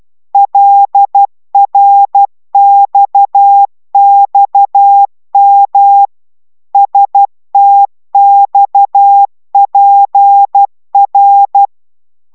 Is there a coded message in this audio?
Yes: Morse code from 0.3 to 11.7 s, heard as a keyed tone, about 12 wpm, 800 Hz. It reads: LRXXM STXPR